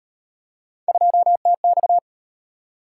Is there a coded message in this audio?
Yes: Morse code from 0.9 to 2.0 s, heard as a keyed tone, about 33 wpm, 700 Hz. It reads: 2TX